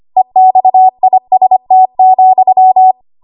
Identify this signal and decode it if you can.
Morse code, heard as a keyed tone, about 25 wpm, 750 Hz. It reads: EXIST,
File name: Archive / 25 WPM / 25WPM